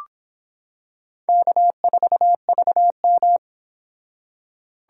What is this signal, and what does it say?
Morse code, heard as a keyed tone, about 26 wpm, 700 Hz. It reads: K4VM